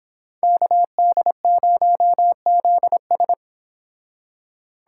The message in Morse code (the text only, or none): KD0ZS